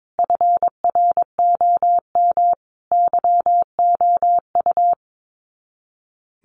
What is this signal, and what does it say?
Morse code, heard as a keyed tone, about 22 wpm, 700 Hz. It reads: FROM YOU
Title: Common QSO Phrases 22wpm